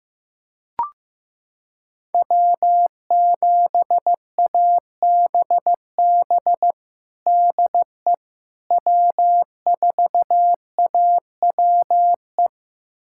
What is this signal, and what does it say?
Morse code, heard as a keyed tone, about 15 wpm, 700 Hz. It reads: W7ABB DE W4AWE
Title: USA Callsign DE Callsign - 5 Characters - 15wpm